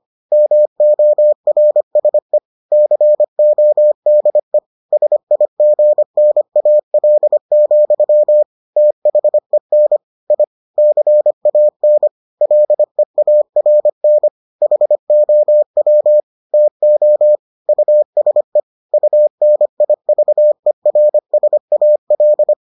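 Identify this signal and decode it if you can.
Morse code, heard as a keyed tone, about 25 wpm, 600 Hz. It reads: MORSE CODE SIGNAL, THEN I CAN LEARN HOW TO USE UNIVERSAL
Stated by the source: Morse edition of the Foundations of Amateur Radio episodes.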